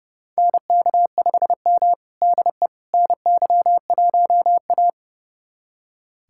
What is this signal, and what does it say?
Morse code, heard as a keyed tone, about 30 wpm, 700 Hz. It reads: NK5M DE NY1A
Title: USA Callsign DE Callsign - 4 Characters - 30wpm